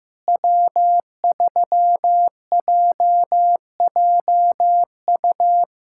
Morse code, heard as a keyed tone, about 16 wpm, 700 Hz. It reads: W3JJU
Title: USA Callsign DE Callsign 15wpm